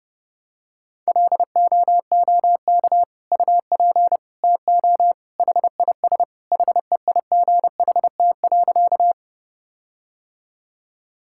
Morse code, heard as a keyed tone, about 30 wpm, 700 Hz. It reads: LOOK UP TO HIS HEIGHT.
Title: Sentences from Top 1600 Words 30wpm